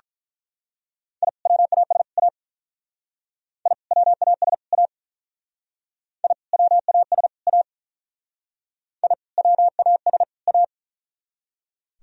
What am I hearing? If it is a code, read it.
Morse code, heard as a keyed tone, about 44 wpm, 700 Hz. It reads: I WAS A I WAS A I WAS A I WAS A